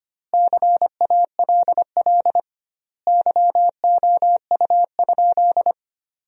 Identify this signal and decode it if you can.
Morse code, heard as a keyed tone, about 25 wpm, 700 Hz. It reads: CALL YOU?
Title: 2x Word-Spacing - Sentences from Top 100 Words - 25wpm